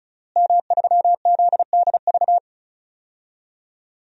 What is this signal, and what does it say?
Morse code, heard as a keyed tone, about 36 wpm, 700 Hz. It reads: M3ZDV
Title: DXCC-100 Call Signs - 5 Characters - Repeated 3x - 35wpm